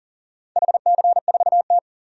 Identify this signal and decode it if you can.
Morse code, heard as a keyed tone, about 37 wpm, 700 Hz. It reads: HC4T
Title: DXCC-100 Call Signs - 4 Characters - 40wpm